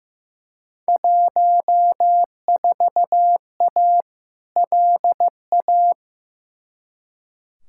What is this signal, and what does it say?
Morse code, heard as a keyed tone, about 15 wpm, 700 Hz. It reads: 14A LA
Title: Field Day Exchanges Encore - 15wpm